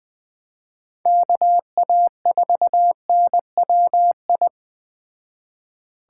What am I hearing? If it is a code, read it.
Morse code, heard as a keyed tone, about 20 wpm, 700 Hz. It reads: KA4NWI
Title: Speed Racing - USA 2x3 Call Signs 17wpm